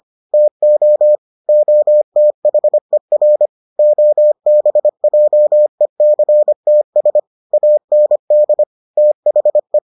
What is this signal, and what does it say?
Morse code, heard as a keyed tone, about 25 wpm, 600 Hz. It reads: TO OTHER OBJECTS AND THE